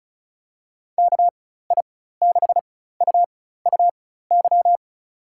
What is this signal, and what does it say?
Morse code, heard as a keyed tone, about 36 wpm, spaced as for 16 wpm, 700 Hz. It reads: KI6UUY
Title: USA Call Signs - 6 Characters - 4x Character Spacing - 35wpm